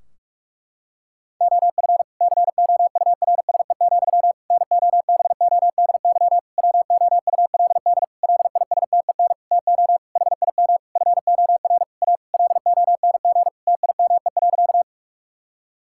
Morse code, heard as a keyed tone, about 41 wpm, 700 Hz. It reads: OF COURSE, NOBODY WOULD LISTEN TO HIM FOR A LONG TIME.
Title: Sentences from Top 1600 Words 45wpm